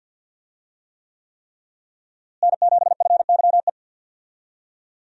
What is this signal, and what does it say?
Morse code, heard as a keyed tone, about 50 wpm, 700 Hz. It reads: N7FYE
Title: USA Call Signs - 5 Characters - Repeated 3x - 50wpm